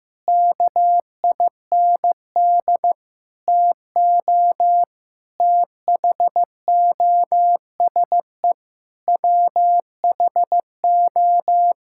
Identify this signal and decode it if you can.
Morse code, heard as a keyed tone, about 15 wpm, 700 Hz. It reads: KIND TO THOSE WHO
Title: Sentences from Top 1600 Words 15wpm